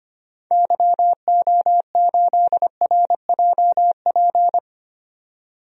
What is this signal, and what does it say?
Morse code, heard as a keyed tone, about 25 wpm, 700 Hz. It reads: YO8RJP
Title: DXCC-100 Call Signs - 6 Characters - 25wpm